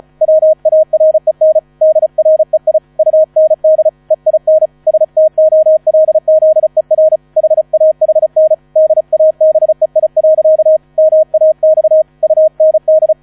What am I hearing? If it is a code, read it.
Morse code, heard as a keyed tone, about 35 wpm, 630 Hz. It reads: WAREN DREI UND EIN STOLZER HAHN DABEI. MAX UND
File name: 35wpm